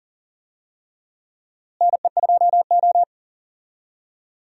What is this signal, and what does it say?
Morse code, heard as a keyed tone, about 40 wpm, 700 Hz. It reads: NE2O
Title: USA Call Signs - 4 Characters - Repeated 3x - 40wpm